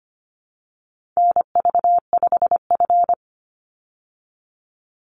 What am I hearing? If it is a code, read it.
Morse code, heard as a keyed tone, about 25 wpm, 700 Hz. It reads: NV5F
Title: Call Signs - 25wpm